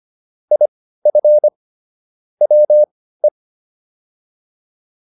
Morse code, heard as a keyed tone, about 25 wpm, spaced as for 15 wpm, 600 Hz. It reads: IF WE